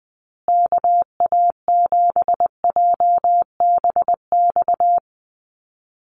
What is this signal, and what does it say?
Morse code, heard as a keyed tone, about 20 wpm, 700 Hz. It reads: KA7JBX
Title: USA Call Signs Repeated 3x 20wpm